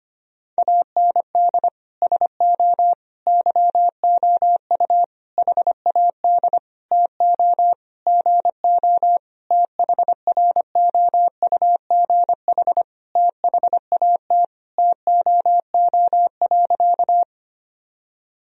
Morse code, heard as a keyed tone, about 25 wpm, 700 Hz. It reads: AND SO YOU HAD TO GO THROUGH THAT TOO.
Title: Sentences from Top 1000 Words - Review Part 1 - 25wpm